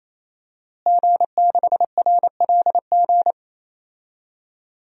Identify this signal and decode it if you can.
Morse code, heard as a keyed tone, about 28 wpm, 700 Hz. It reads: G6RLG
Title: DXCC-100 Call Signs - 5 Characters - Repeated 3x - 28wpm